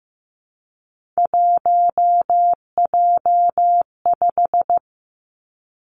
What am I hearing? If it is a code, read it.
Morse code, heard as a keyed tone, about 15 wpm, 700 Hz. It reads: 1J5
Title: Three Challenging Letter-Number Combinations 15wpm